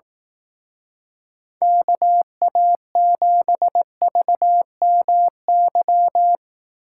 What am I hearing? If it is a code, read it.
Morse code, heard as a keyed tone, about 18 wpm, 700 Hz. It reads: KA7VMY
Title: Speed Racing - USA Call Signs 15wpm